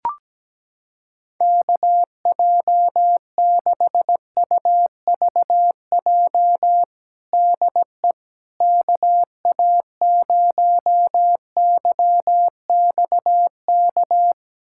Morse code, heard as a keyed tone, about 17 wpm, 700 Hz. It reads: KJ6UVJ DE KA0YXK
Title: USA Callsign DE Callsign - 6 Characters - 17wpm